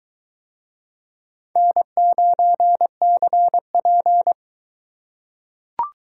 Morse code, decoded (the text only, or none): N9CP